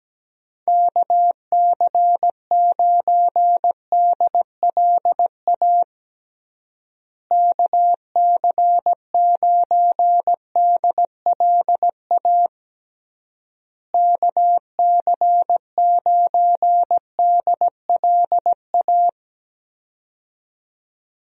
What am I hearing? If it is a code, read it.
Morse code, heard as a keyed tone, about 17 wpm, 700 Hz. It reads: KC9DLA KC9DLA KC9DLA
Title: USA Call Signs - 6 Characters - Repeated 3x - 17wpm